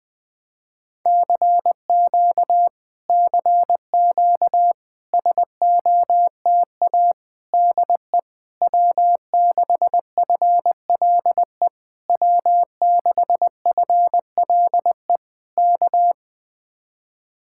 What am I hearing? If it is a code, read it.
Morse code, heard as a keyed tone, about 20 wpm, 700 Hz. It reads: CQ CQ SOTA DE W6FLE W6FLE K
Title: Speed-Racing Common QSO Phrases - 17wpm